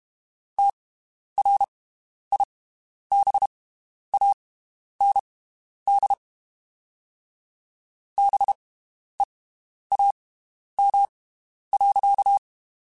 Morse code, heard as a keyed tone, about 32 wpm, spaced as for 11 wpm, 800 Hz. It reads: TRIBAND BEAM.
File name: (spaced) / QSOs Spaced-mp3s / WD0BGP 1 @30 sp